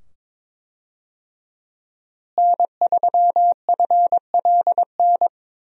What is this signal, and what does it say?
Morse code, heard as a keyed tone, about 22 wpm, 700 Hz. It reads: N3FLN
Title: USA Call Signs - 5 Characters - Repeated 2x - 22wpm